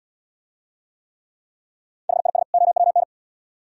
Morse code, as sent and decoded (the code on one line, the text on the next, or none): .... .. ... / --- .-- -.
HIS OWN